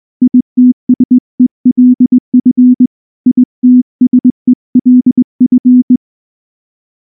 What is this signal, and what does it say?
Morse code, heard as a keyed tone, about 20 wpm, 250 Hz. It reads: ITSELF ITSELF
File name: morse-training-20WPM